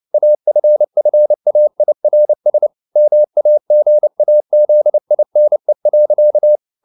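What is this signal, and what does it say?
Morse code, heard as a keyed tone, about 29 wpm, 600 Hz. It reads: AFFAIRS MAGAZINE.
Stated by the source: Morse code at 29 wpm, sending Ken Follett's novel Never in English, from CW med Gnister.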